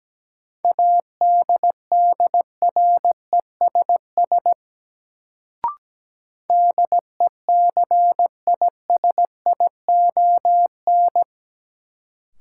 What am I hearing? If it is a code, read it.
Morse code, heard as a keyed tone, about 17 wpm, 700 Hz. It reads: ADDRESS DECISION
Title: Sentences from Top 1000 Words - New Word Drill - 17wpm